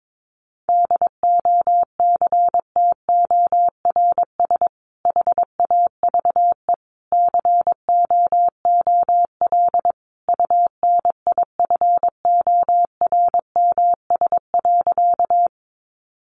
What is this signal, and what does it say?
Morse code, heard as a keyed tone, about 22 wpm, 700 Hz. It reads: DOCTORS HAVE COOL UNIFORMS.
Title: Easy Phrases 22wpm